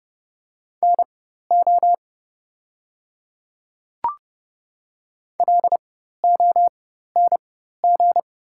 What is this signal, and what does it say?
Morse code, heard as a keyed tone, about 30 wpm, spaced as for 14 wpm, 690 Hz. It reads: NO LONG
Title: 4x Character Spacing - Top 100 Words - 30wpm